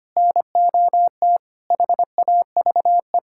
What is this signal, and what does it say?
Morse code, heard as a keyed tone, about 25 wpm, 700 Hz. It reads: NOT HAVE
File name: practice / 25wpm / Sentences with 3-Word Phrases - 25wpm